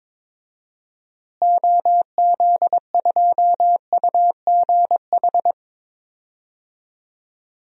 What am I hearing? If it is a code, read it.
Morse code, heard as a keyed tone, about 22 wpm, 700 Hz. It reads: OZ2UGH
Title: DXCC-100 Call Signs - 6 Characters - Repeated 3x - 22wpm